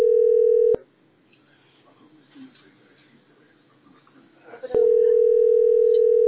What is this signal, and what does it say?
Ringback tone (North America), 440 + 480 Hz.